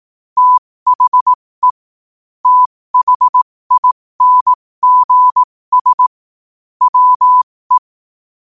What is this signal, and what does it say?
Morse code, heard as a keyed tone, about 18 wpm, spaced as for 15 wpm, 1000 Hz. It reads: THE THINGS WE
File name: qotdfiles / 15wpm